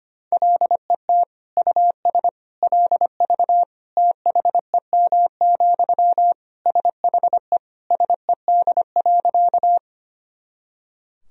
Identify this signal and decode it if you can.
Morse code, heard as a keyed tone, about 25 wpm, 700 Hz. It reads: LET US LV THEM, SHE SED.
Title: Sentences from Top 400 Words and Abbreviations 25wpm